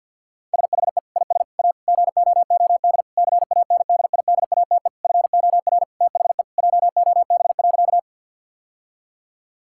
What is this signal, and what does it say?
Morse code, heard as a keyed tone, about 50 wpm, 700 Hz. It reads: SHE IS A GOOD CANDIDATE FOR THE JOB.
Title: Sentences from Top 1600 Words 50wpm